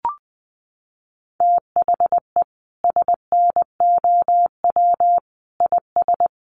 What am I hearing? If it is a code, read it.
Morse code, heard as a keyed tone, about 20 wpm, 700 Hz. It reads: THE SNOW IS
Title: Sentences from Top 500 Words 20wpm